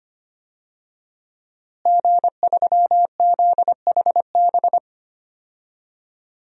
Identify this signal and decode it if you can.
Morse code, heard as a keyed tone, about 25 wpm, 700 Hz. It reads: G3ZHB
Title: DXCC-100 Call Signs - 5 Characters - Repeated 3x - 25wpm